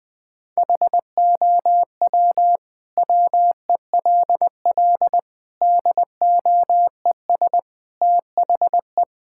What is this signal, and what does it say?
Morse code, heard as a keyed tone, about 20 wpm, 700 Hz. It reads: HOW WELL DOES THE
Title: Sentences from Top 1000 Words - Review Part 6 - 20wpm